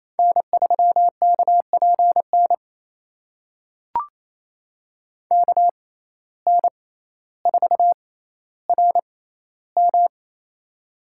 Morse code, heard as a keyed tone, about 29 wpm, spaced as for 15 wpm, 700 Hz. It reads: N3KPN KN4RM
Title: USA Call Signs - 5 Characters - 6x Character Spacing - 28wpm